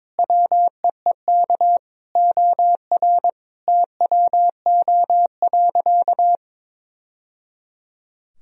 Morse code, heard as a keyed tone, about 22 wpm, 700 Hz. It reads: WEEK OR TWO.
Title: Sentences from Top 1000 Words - Review Part 2 - 22wpm